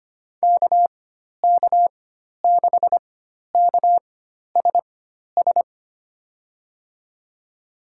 Morse code, heard as a keyed tone, about 25 wpm, spaced as for 12 wpm, 700 Hz. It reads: KK6KSS
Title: USA Call Signs - 6 Characters - 4x Character Spacing - 25wpm